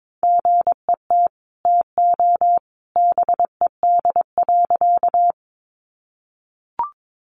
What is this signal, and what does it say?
Morse code, heard as a keyed tone, about 22 wpm, 700 Hz. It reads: GET TO BED.
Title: Sentences from Top 500 Words 22wpm